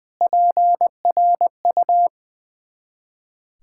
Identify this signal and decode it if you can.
Morse code, heard as a keyed tone, about 20 wpm, 700 Hz. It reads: PRU
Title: Ontario QSO Party Exchange 20wpm